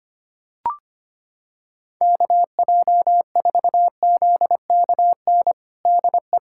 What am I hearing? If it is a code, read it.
Morse code, heard as a keyed tone, about 25 wpm, 700 Hz. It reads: KJ4ZKN DE
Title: USA Callsign DE Callsign - 6 Characters - 25wpm